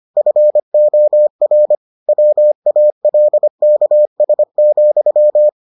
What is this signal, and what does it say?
Morse code, heard as a keyed tone, about 25 wpm, 600 Hz. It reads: FOR WALKS,